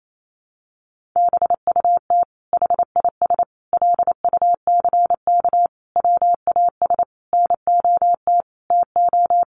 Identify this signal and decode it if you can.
Morse code, heard as a keyed tone, about 28 wpm, 700 Hz. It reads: BUT HIS LUCK WAS NOT TO
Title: Sentences from Top 1500 Words 28wpm